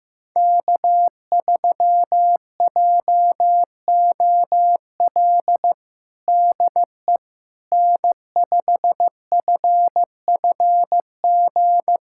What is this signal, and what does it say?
Morse code, heard as a keyed tone, about 15 wpm, 700 Hz. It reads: K3JOL DE N5FFG
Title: USA Callsign DE Callsign - 5 Characters - 15wpm